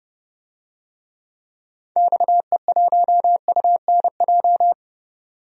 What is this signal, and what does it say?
Morse code, heard as a keyed tone, about 30 wpm, 700 Hz. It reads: XE1UNJ
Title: DXCC-100 Call Signs - 6 Characters - 30wpm